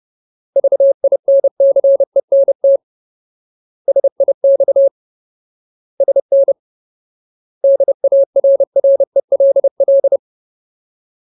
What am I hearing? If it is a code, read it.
Morse code, heard as a keyed tone, about 30 wpm, 550 Hz. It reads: VINCENT SIX SN DARRELL
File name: Ultimate Word List / 30WPM Ultimate